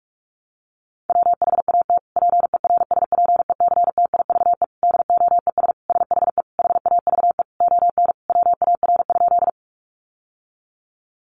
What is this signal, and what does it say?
Morse code, heard as a keyed tone, about 44 wpm, 700 Hz. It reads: WHAT PERSPECTIVE DOES SHE HAVE ON WAR?
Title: Sentences from Top 1200 Words 45wpm